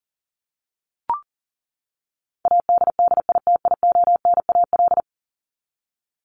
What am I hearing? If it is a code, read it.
Morse code, heard as a keyed tone, about 40 wpm, 700 Hz. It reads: ADDITIONAL
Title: Top 501-750 Words 40wpm